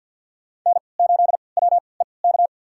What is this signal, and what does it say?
Morse code, heard as a keyed tone, about 52 wpm, spaced as for 28 wpm, 700 Hz. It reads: N8WEK
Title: USA Call Signs - 3x Character Spacing - 50wpm